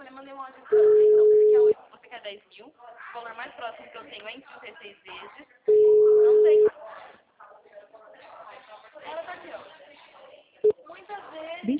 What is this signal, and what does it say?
Ringback tone (Europe, 425 Hz), with speech.